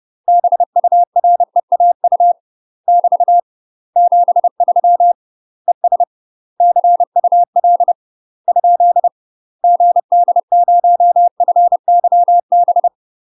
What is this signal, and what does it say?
Morse code, heard as a keyed tone, about 30 wpm, 700 Hz. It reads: BUREAU <BT> 73 ES CUL ? GD0FYB
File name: UK MASTER QSOs TRAINING/30WPM